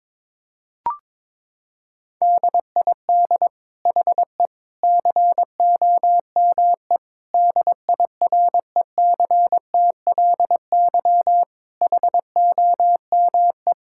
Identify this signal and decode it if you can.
Morse code, heard as a keyed tone, about 22 wpm, 700 Hz. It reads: DID HE COME DIRECTLY HOME
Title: Sentences from Top 1000 Words - Review Part 5 - 22wpm